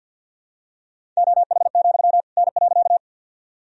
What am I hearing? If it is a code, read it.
Morse code, heard as a keyed tone, about 51 wpm, 700 Hz. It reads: OH, N.
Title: Sentences from Top 500 Words and Abbreviations 50wpm